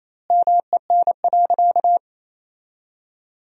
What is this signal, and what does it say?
Morse code, heard as a keyed tone, about 28 wpm, 700 Hz. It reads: MEN.
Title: 2x Word-Spacing - Sentences from Top 300 Words - 28wpm